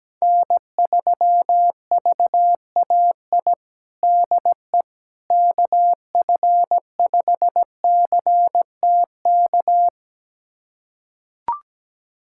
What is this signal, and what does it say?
Morse code, heard as a keyed tone, about 17 wpm, 700 Hz. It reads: N3VAI DE KF5CTK
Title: USA Callsign DE Callsign 17wpm